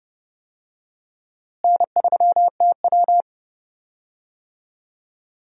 Morse code, heard as a keyed tone, about 32 wpm, 700 Hz. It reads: N3TW
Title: USA Call Signs - 4 Characters - Repeated 3x - 30wpm